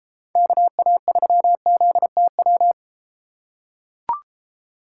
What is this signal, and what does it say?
Morse code, heard as a keyed tone, about 31 wpm, 700 Hz. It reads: KA3ZTW